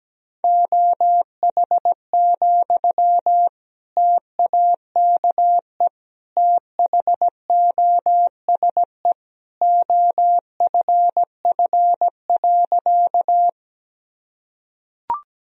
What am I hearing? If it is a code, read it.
Morse code, heard as a keyed tone, about 17 wpm, 700 Hz. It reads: OH, TAKE THOSE OFF.
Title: Sentences from Top 1000 Words - Review Part 1 - 17wpm